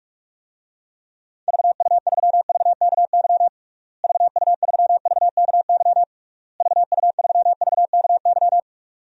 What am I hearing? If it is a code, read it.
Morse code, heard as a keyed tone, about 46 wpm, 700 Hz. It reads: VU3VKY VU3VKY VU3VKY
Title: DXCC-100 Call Signs - 6 Characters - Repeated 3x - 45wpm